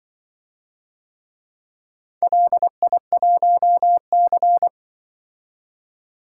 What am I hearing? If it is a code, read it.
Morse code, heard as a keyed tone, about 24 wpm, 700 Hz. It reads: LI1C
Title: Speed-Racing - Four Letter-Number Combinations - 20wpm